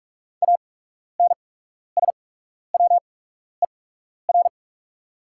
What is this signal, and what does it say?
Morse code, heard as a keyed tone, about 44 wpm, spaced as for 12 wpm, 690 Hz. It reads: ANSWER
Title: 8x Character Spacing - Top 300 Words - 45wpm